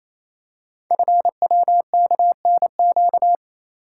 Morse code, heard as a keyed tone, about 27 wpm, 690 Hz. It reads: FWKNQ
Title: Code Groups - 5 Letters - 28wpm